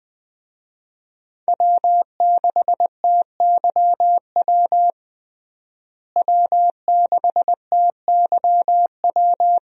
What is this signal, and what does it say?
Morse code, heard as a keyed tone, about 20 wpm, 700 Hz. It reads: W6TYW W6TYW
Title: USA Call Signs - 5 Characters - Repeated 2x - 20wpm